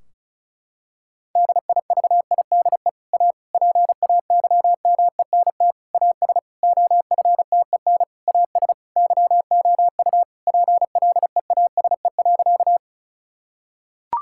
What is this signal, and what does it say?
Morse code, heard as a keyed tone, about 36 wpm, 700 Hz. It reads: DIVIDE A PAYMENT AS OFTEN AS YOU PLEASE.
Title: Sentences from Top 1600 Words 35wpm